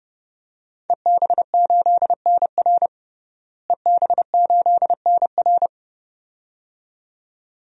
Morse code, heard as a keyed tone, about 32 wpm, 700 Hz. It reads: EB8NR EB8NR
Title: DXCC-100 Call Signs - 5 Characters - Repeated 2x - 30wpm